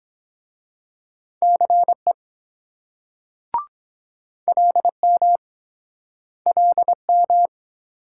Morse code, heard as a keyed tone, about 24 wpm, 700 Hz. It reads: CE LM LM